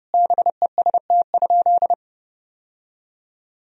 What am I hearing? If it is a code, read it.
Morse code, heard as a keyed tone, about 30 wpm, 700 Hz. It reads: BEST?
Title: Sentences from Top 1000 Words - Review Part 2 - 30wpm